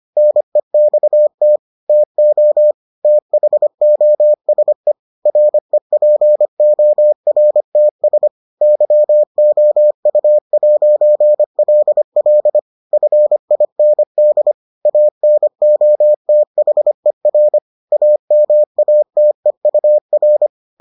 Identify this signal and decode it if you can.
Morse code, heard as a keyed tone, about 25 wpm, 600 Hz. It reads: NEXT TO THOSE REPORTS YOU'LL FIND ANOTHER AMATEUR